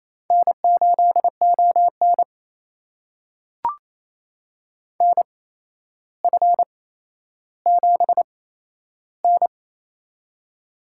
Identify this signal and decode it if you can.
Morse code, heard as a keyed tone, about 28 wpm, spaced as for 12 wpm, 700 Hz. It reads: N8ON NF7N
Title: USA Call Signs - 4 Characters - 8x Character Spacing - 28wpm